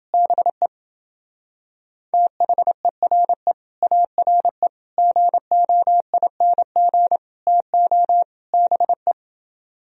Morse code, heard as a keyed tone, about 28 wpm, 700 Hz. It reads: BE THERE ARE GOING TO BE